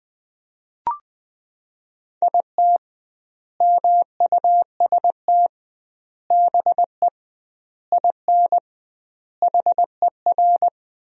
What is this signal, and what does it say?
Morse code, heard as a keyed tone, about 21 wpm, 700 Hz. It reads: IT MUST BE IN HER
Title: 2x Word-Spacing - Sentences from Top 300 Words - 20wpm